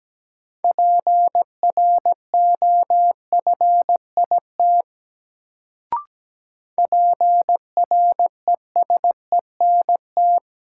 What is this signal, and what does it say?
Morse code, heard as a keyed tone, about 17 wpm, 700 Hz. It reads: PROFIT PRESENT
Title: Sentences from Top 800 Words - New Word Drill - 17wpm